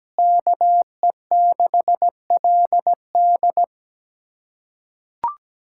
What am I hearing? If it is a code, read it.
Morse code, heard as a keyed tone, about 17 wpm, 700 Hz. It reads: KE6LD
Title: USA Callsign DE Callsign - 5 Characters - 17wpm